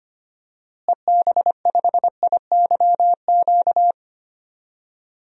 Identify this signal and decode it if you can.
Morse code, heard as a keyed tone, about 25 wpm, 700 Hz. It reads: EB5IYQ
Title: DXCC-100 Call Signs - 6 Characters - Repeated 2x - 25wpm